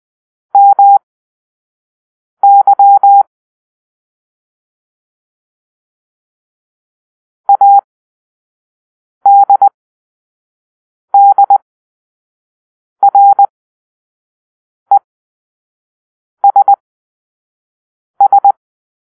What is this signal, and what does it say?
Morse code, heard as a keyed tone, about 20 wpm, spaced as for 5 wpm, 800 Hz. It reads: MY ADDRESS
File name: arrl / Grp-6 Drill-4